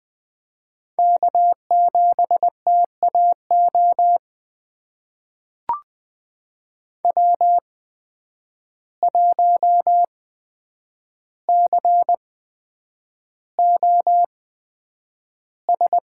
Morse code, heard as a keyed tone, about 20 wpm, spaced as for 9 wpm, 700 Hz. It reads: K7TAO W1COS
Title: USA Call Signs - 5 Characters - 8x Character Spacing - 20wpm